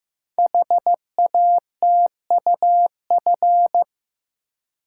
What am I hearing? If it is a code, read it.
Morse code, heard as a keyed tone, about 15 wpm, 700 Hz. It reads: HATUF
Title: Code Groups - 5 Letters - 15wpm